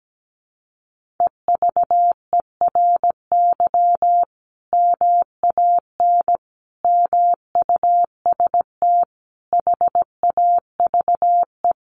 Morse code, heard as a keyed tone, about 17 wpm, 700 Hz. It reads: EVERY MAN MUST HAVE
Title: Sentences from Top 1400 Words 17wpm